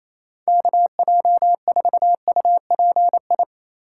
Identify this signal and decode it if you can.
Morse code, heard as a keyed tone, about 37 wpm, 700 Hz. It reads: KJ4UPI